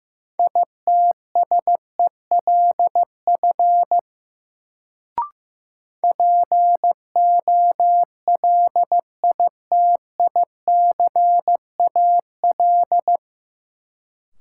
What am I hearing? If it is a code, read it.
Morse code, heard as a keyed tone, about 15 wpm, 700 Hz. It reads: ITSELF POLITICAL
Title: Sentences from Top 800 Words - New Word Drill - 15wpm